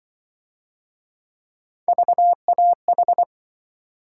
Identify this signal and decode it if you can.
Morse code, heard as a keyed tone, about 25 wpm, 700 Hz. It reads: VAH
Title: Speed-Racing - Three Letter-Number Combinations - 20wpm